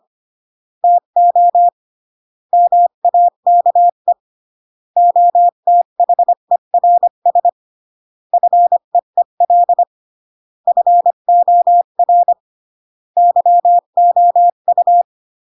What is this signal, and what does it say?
Morse code, heard as a keyed tone, about 25 wpm, 700 Hz. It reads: TO MAKE OTHERS FEEL FOR YOU
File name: CW / jodorowsky / 82 commandments 25-20